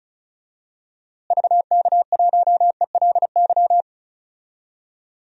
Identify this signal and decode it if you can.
Morse code, heard as a keyed tone, about 36 wpm, 700 Hz. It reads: VK1ELY
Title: DXCC-100 Call Signs - 6 Characters - Repeated 3x - 35wpm